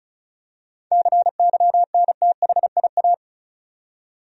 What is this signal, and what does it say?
Morse code, heard as a keyed tone, about 34 wpm, 700 Hz. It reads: CYNTHIA
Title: USA Common Names - 35wpm